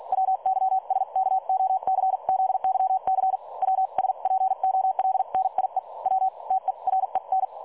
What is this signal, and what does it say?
Morse code, heard as a keyed tone, about 48 wpm, spaced as for 40 wpm, 760 Hz. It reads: WYSKOCZYC WSPOLNIE W TEREN
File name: dlazaawansowanych / nagraniazpasma / websdr recording 3565.0kHz